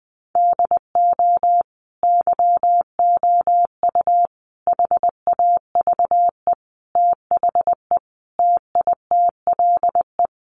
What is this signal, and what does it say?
Morse code, heard as a keyed tone, about 20 wpm, 700 Hz. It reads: DO YOU HAVE THE TITLE
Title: Sentences from Top 1200 Words 20wpm